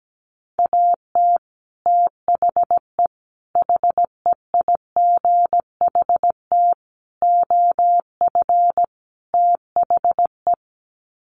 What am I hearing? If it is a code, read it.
Morse code, heard as a keyed tone, about 17 wpm, 700 Hz. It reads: AT THE HEIGHT OF THE
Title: Sets of 5 Words 17wpm